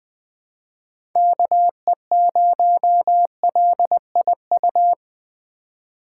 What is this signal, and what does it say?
Morse code, heard as a keyed tone, about 20 wpm, 700 Hz. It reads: KE0LIU
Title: USA Call Signs - 6 Characters - 4x Character Spacing - 20wpm